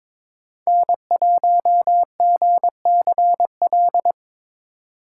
Morse code, heard as a keyed tone, about 22 wpm, 700 Hz. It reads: N1GCL
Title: USA Call Signs - 2x Character Spacing - 22wpm